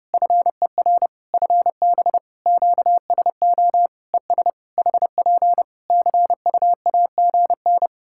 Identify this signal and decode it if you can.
Morse code, heard as a keyed tone, about 30 wpm, 700 Hz. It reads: FER FB QSO ES HP CUAGN
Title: Practice QSO - Exchange 4 - Ending - 30wpm